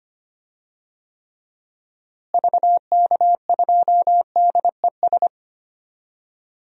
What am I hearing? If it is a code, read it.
Morse code, heard as a keyed tone, about 25 wpm, 700 Hz. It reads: VK2DES
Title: DXCC-100 Call Signs - 6 Characters - Repeated 2x - 25wpm